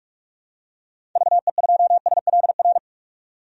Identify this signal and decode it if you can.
Morse code, heard as a keyed tone, about 44 wpm, 700 Hz. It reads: VE2SLR
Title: DXCC-100 Call Signs - 6 Characters - 45wpm